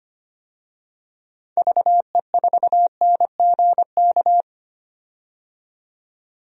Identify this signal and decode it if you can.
Morse code, heard as a keyed tone, about 25 wpm, 700 Hz. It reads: VE4NGK
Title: DXCC-100 Call Signs - 6 Characters - 25wpm